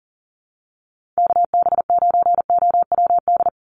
Morse code, heard as a keyed tone, about 40 wpm, 700 Hz. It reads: KB9OWD
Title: Call Signs - 40wpm